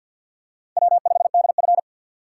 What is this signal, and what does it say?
Morse code, heard as a keyed tone, about 52 wpm, 700 Hz. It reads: W5DF
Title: USA Call Signs - 4 Characters - 2x Character Spacing - 50wpm